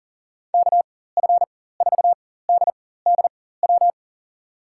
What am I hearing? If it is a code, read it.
Morse code, heard as a keyed tone, about 37 wpm, spaced as for 18 wpm, 700 Hz. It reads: KF4DDW
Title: USA Call Signs - 6 Characters - 4x Character Spacing - 40wpm